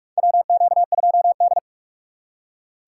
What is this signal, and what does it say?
Morse code, heard as a keyed tone, about 43 wpm, 700 Hz. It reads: WQ2D
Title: USA Call Signs - 4 Characters - 2x Character Spacing - 45wpm